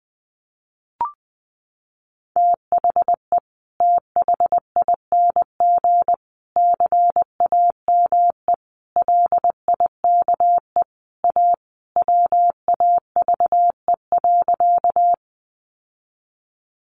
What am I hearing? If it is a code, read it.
Morse code, heard as a keyed tone, about 19 wpm, 700 Hz. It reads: THE THING CAME LIKE A WAVE.
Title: Sentences from Top 1400 Words 20wpm